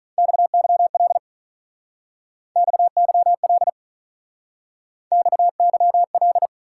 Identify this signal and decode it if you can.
Morse code, heard as a keyed tone, about 46 wpm, 700 Hz. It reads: XYL XYL XYL